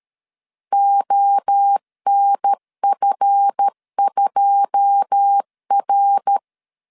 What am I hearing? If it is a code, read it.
Morse code, heard as a keyed tone, about 13 wpm, 800 Hz. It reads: ONF2R
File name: CW / DET / 1 9 1500C S B 12wpm 3819